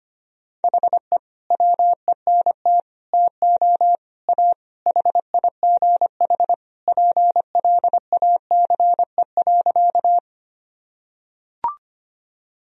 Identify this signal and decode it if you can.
Morse code, heard as a keyed tone, about 25 wpm, 700 Hz. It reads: HE WENT TO A HIGH PLACE.
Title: Sentences from Top 1000 Words - Review Part 1 - 25wpm